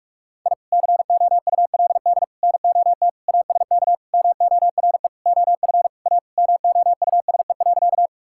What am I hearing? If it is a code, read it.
Morse code, heard as a keyed tone, about 41 wpm, 700 Hz. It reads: I COULD NOT ASK MORE OF A MOUSE.
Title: Sentences from Top 1000 Words - Review Part 5 - 45wpm